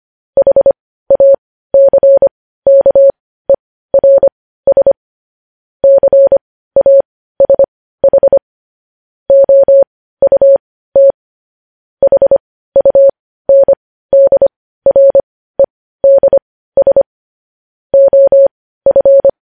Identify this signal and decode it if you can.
Morse code, heard as a keyed tone, about 25 wpm, spaced as for 15 wpm, 550 Hz. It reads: HACKERS CASH OUT HUNDREDS OF
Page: News headlines in Morse code at 15 wpm